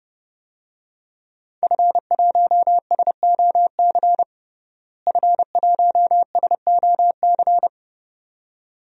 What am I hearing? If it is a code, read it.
Morse code, heard as a keyed tone, about 30 wpm, 700 Hz. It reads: F1SOC F1SOC
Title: DXCC-100 Call Signs - 5 Characters - Repeated 2x - 30wpm